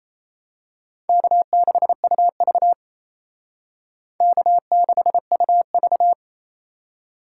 Morse code, heard as a keyed tone, about 30 wpm, 700 Hz. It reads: K6UV K6UV
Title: Speed Racing - USA 1x2 Call Signs 28wpm